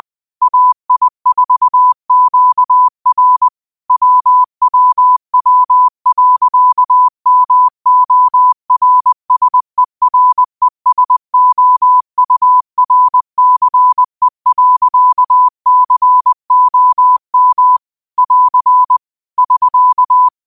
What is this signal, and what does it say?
Morse code, heard as a keyed tone, about 20 wpm, 1000 Hz. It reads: AI4QR WWW.MORSERESOURCE.COM <AR> <SK>
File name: qotdfiles / 20wpm